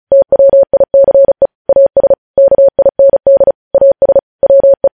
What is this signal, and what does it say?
Morse code, heard as a keyed tone, about 36 wpm, 550 Hz. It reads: TWICE AS KIND AS WE
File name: podcast / News headlines-1x-35x35